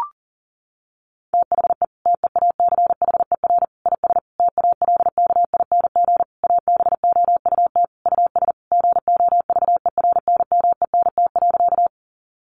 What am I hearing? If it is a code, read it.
Morse code, heard as a keyed tone, about 40 wpm, 700 Hz. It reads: THE TEACHER IS TALKING ABOUT US GOVERNMENT.
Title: Easy Phrases 40wpm